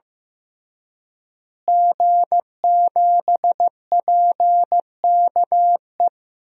Morse code, heard as a keyed tone, about 15 wpm, 700 Hz. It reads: G7PKE
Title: DXCC-100 Call Signs - 5 Characters - Repeated 3x - 15wpm